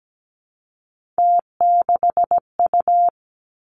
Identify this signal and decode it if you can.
Morse code, heard as a keyed tone, about 17 wpm, 700 Hz. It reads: T6U
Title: Three Challenging Letter-Number Combinations 17wpm